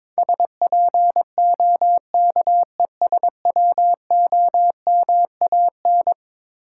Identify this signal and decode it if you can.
Morse code, heard as a keyed tone, about 22 wpm, 700 Hz. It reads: SPOKESWOMAN
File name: practice / 22wpm / Eleven Letter Words - 22wpm